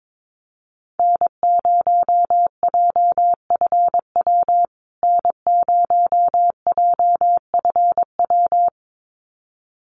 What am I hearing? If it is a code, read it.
Morse code, heard as a keyed tone, about 22 wpm, 700 Hz. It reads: N0JFW N0JFW
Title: USA Call Signs Repeated 2x 22wpm